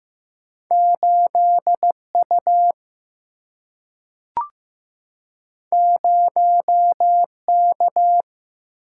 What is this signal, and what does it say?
Morse code, heard as a keyed tone, about 15 wpm, 700 Hz. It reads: 8U 0K
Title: Single-Letter Single-Number 15wpm